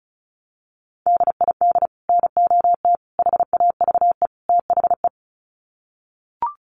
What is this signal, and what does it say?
Morse code, heard as a keyed tone, about 34 wpm, 700 Hz. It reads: DID NOT HAVE THE
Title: Sets of 4 Words 35wpm